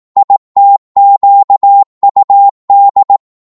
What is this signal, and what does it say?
Morse code, heard as a keyed tone, about 18 wpm, 800 Hz. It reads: ITQUD
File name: Morse / ltr-18wpm